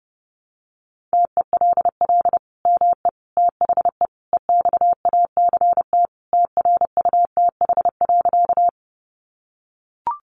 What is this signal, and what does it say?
Morse code, heard as a keyed tone, about 30 wpm, 700 Hz. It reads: TELL ME THE EXACT TRUTH.
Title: Sentences from Top 1300 Words 30wpm